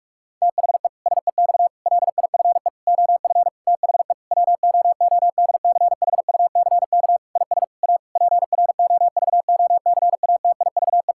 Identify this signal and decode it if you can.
Morse code, heard as a keyed tone, about 46 wpm, 700 Hz. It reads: THE SEX LIFE OF THE WOODCHUCK IS A PROVOCATIVE
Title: Pangram Challenge v2 - 45wpm